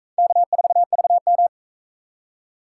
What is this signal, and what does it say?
Morse code, heard as a keyed tone, about 38 wpm, 700 Hz. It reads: K4VM